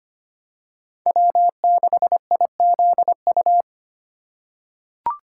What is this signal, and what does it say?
Morse code, heard as a keyed tone, about 25 wpm, 700 Hz. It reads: W6IZU
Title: USA Call Signs - 5 Characters - 3x Character Spacing - 25wpm